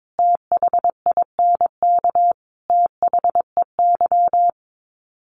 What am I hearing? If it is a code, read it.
Morse code, heard as a keyed tone, about 22 wpm, 700 Hz. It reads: THINK THEY
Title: Sets of 5 Words 22wpm